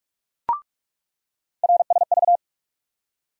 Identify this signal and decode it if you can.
Morse code, heard as a keyed tone, about 46 wpm, 700 Hz. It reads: RSV
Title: USA Call Sign Postfixes - 3 Letters 45wpm